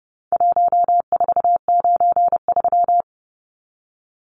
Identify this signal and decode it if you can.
Morse code, heard as a keyed tone, about 30 wpm, 700 Hz. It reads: 1493
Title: Numbers - Contest Style - 4-Digit 30wpm